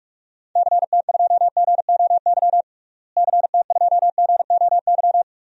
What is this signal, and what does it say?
Morse code, heard as a keyed tone, about 46 wpm, 700 Hz. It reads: CT2GOY CT2GOY
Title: DXCC-100 Call Signs - 6 Characters - Repeated 2x - 45wpm